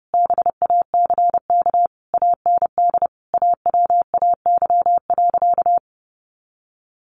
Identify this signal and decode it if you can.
Morse code, heard as a keyed tone, about 30 wpm, 700 Hz. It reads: BACK AND AWAY.
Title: A Sound Of Thunder 30wpm